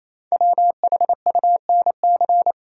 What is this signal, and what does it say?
Morse code, heard as a keyed tone, about 28 wpm, 700 Hz. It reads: WHUNC
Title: Code Groups - 5 Letters - 28wpm